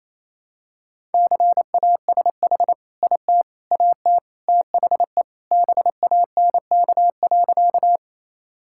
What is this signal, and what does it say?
Morse code, heard as a keyed tone, about 27 wpm, 700 Hz. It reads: CASH IT AT THE BANK.
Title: Sentences from Top 1000 Words - Review Part 5 - 28wpm